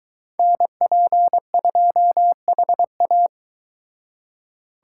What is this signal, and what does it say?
Morse code, heard as a keyed tone, about 23 wpm, 700 Hz. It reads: NP2HA